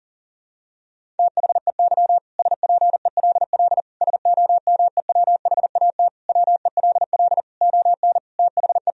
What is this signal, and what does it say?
Morse code, heard as a keyed tone, about 40 wpm, 700 Hz. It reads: THEY SPELL SOMEWHAT WELL ON THE
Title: Sentences from Top 1000 Words - Review Part 1 - 40wpm